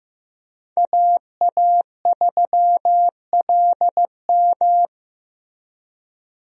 Morse code, heard as a keyed tone, about 15 wpm, 700 Hz. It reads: AA3LM
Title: USA Call Signs - 5 Characters - Repeated 2x - 15wpm